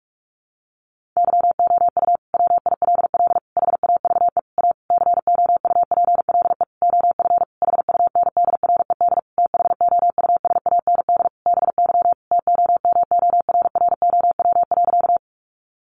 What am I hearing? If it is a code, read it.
Morse code, heard as a keyed tone, about 44 wpm, 700 Hz. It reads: YOU WILL HAVE A COUPLE OF HUNDRED THOUSAND BY TOMORROW.
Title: Sentences from Top 1100 Words 45wpm